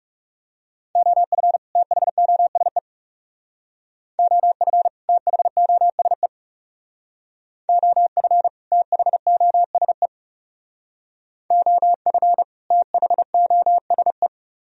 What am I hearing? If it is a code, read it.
Morse code, heard as a keyed tone, about 37 wpm, 700 Hz. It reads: OF THOSE OF THOSE OF THOSE OF THOSE
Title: Speed-Racing - Sets of 2 Words 30wpm